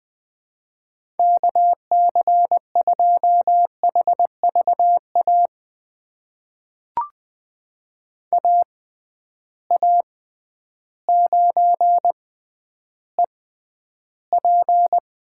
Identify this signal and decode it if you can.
Morse code, heard as a keyed tone, about 20 wpm, spaced as for 11 wpm, 700 Hz. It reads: KC2HVA AA9EP
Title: USA Call Signs - 6x Character Spacing - 20wpm